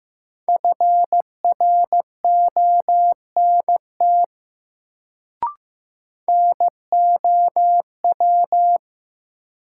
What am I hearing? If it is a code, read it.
Morse code, heard as a keyed tone, about 15 wpm, 700 Hz. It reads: FRONT NOW
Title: Sets of 2 Words - New Word Drill - 15wpm